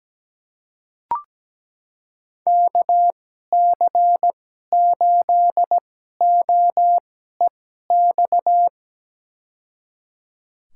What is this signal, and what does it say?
Morse code, heard as a keyed tone, about 17 wpm, spaced as for 12 wpm, 700 Hz. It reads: KC8OEX
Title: USA Call Signs - 6 Characters - 2x Character Spacing - 17wpm